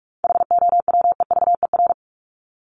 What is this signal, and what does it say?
Morse code, heard as a keyed tone, about 44 wpm, 700 Hz. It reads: HOWEVER
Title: Top 501-750 Words 45wpm